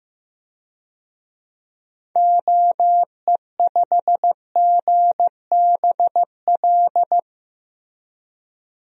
Morse code, heard as a keyed tone, about 15 wpm, 700 Hz. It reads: OE5GBL
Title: DXCC-100 Call Signs - 6 Characters - Repeated 2x - 15wpm